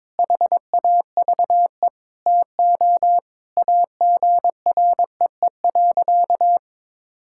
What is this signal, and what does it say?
Morse code, heard as a keyed tone, about 21 wpm, 700 Hz. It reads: HAVE TO AGREE.
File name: practice / 22wpm / Sentences with 4-Word Phrases - 22wpm